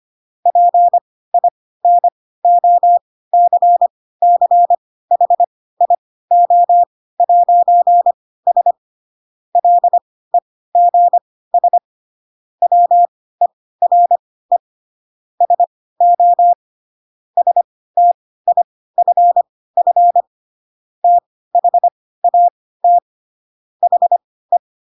Morse code, heard as a keyed tone, about 25 wpm, spaced as for 16 wpm, 700 Hz. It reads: PINOCCHIO'S LEGS WERE SO STIFF THAT HE
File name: Pinocchio / Pinocchio ch3 10 25 ik1hge